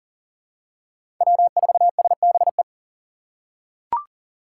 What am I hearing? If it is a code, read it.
Morse code, heard as a keyed tone, about 40 wpm, 700 Hz. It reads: W4SBE
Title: USA Call Signs - 5 Characters - 2x Character Spacing - 40wpm